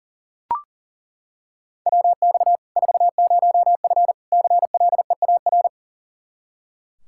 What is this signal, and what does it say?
Morse code, heard as a keyed tone, about 40 wpm, 700 Hz. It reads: WX 40F CLEAR
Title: Practice QSO - Exchange 2 - Info - 40wpm